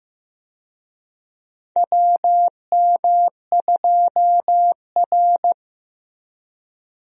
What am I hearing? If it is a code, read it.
Morse code, heard as a keyed tone, about 15 wpm, 700 Hz. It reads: WM2R
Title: USA Call Signs - 4 Characters - Repeated 3x - 15wpm